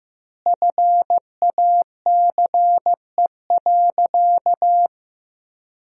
Morse code, heard as a keyed tone, about 15 wpm, 700 Hz. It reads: FACE.
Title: 2x Word-Spacing - Sentences from Top 300 Words - 15wpm